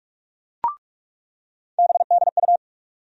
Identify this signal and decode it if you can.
Morse code, heard as a keyed tone, about 43 wpm, 700 Hz. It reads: BDU
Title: USA Call Sign Postfixes - 3 Letters 45wpm